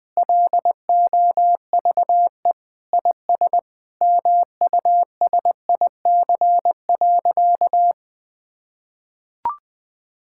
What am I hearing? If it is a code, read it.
Morse code, heard as a keyed tone, about 21 wpm, 700 Hz. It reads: LOVE IS MUSIC.
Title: Sentences with 5-Word Phrases - 20wpm